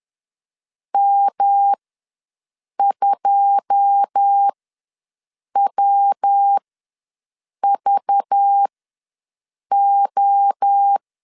Morse code, heard as a keyed tone, about 10 wpm, spaced as for 6 wpm, 800 Hz. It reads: M2WVO
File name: LEZIONI / 10 SIGNS / 10B INT ws 10wpm 4134